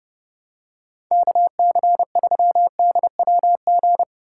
Morse code, heard as a keyed tone, about 30 wpm, 700 Hz. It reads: KC3DWG
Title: USA Call Signs - 6 Characters - 4x Character Spacing - 30wpm